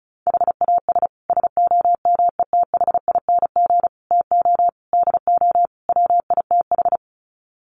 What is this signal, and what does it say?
Morse code, heard as a keyed tone, about 34 wpm, 700 Hz. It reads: HAS SOMETHING TO DO WITH
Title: Sets of 5 Words Encore 35wpm